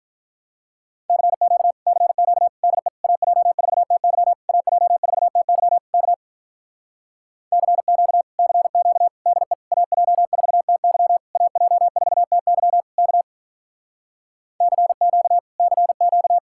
Morse code, heard as a keyed tone, about 47 wpm, 700 Hz. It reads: CQ CQ DE AJ4TY AJ4TY K CQ CQ DE AJ4TY AJ4TY K CQ CQ